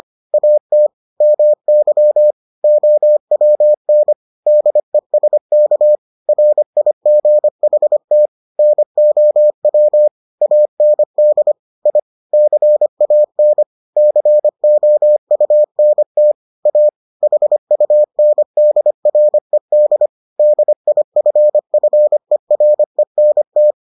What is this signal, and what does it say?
Morse code, heard as a keyed tone, about 25 wpm, 600 Hz. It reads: AT MY OWN DESK RIGHT NOW AND I CAN COUNT A HUNDRED DIFFERENT